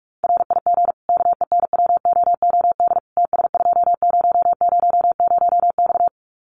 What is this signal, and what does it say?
Morse code, heard as a keyed tone, about 44 wpm, 700 Hz. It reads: RIG KENWOOD TS2000X
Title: Common QSO Phrases 45wpm